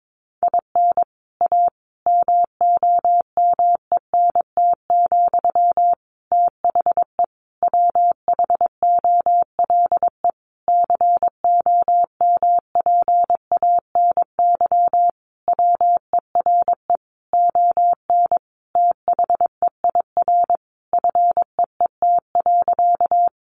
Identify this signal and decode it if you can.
Morse code, heard as a keyed tone, about 22 wpm, 700 Hz. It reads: IN A MOMENT, THE WHOLE COMPANY WERE ON THEIR FEET.
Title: Sentences from Top 1000 Words 22wpm